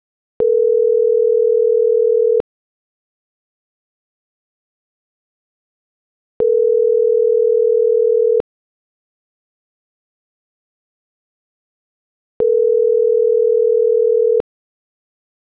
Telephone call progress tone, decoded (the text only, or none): ringback tone